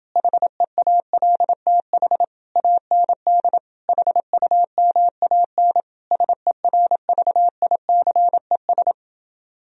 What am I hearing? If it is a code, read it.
Morse code, heard as a keyed tone, about 30 wpm, 700 Hz. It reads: HEALTH AND HUMAN SERVICES